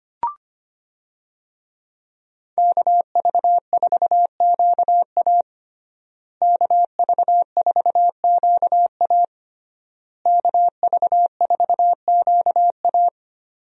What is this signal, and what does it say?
Morse code, heard as a keyed tone, about 25 wpm, 700 Hz. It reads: KV4QA KV4QA KV4QA
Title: USA Call Signs - 5 Characters - Repeated 3x - 25wpm